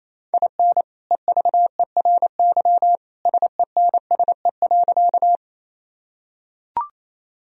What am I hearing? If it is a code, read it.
Morse code, heard as a keyed tone, about 28 wpm, 700 Hz. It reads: IN EVERY SENSE.
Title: Sentences from Top 1000 Words - Review Part 4 - 28wpm